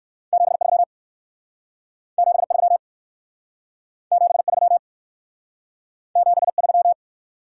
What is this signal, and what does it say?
Morse code, heard as a keyed tone, about 59 wpm, 700 Hz. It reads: 73 73 73 73